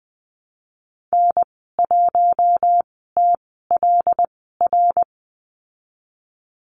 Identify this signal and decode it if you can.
Morse code, heard as a keyed tone, about 20 wpm, spaced as for 14 wpm, 700 Hz. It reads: N1TLR
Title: 076 Lesson 13g - Letter L - Call Signs 20wpm